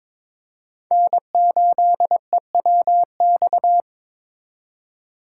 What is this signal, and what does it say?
Morse code, heard as a keyed tone, about 22 wpm, 700 Hz. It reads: N8EWX
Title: K1USN SST QSO Practice - 22wpm